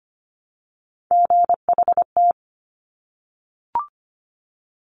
Morse code, heard as a keyed tone, about 25 wpm, 700 Hz. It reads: GHT
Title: Top Three Letter Combinations in Words 25wpm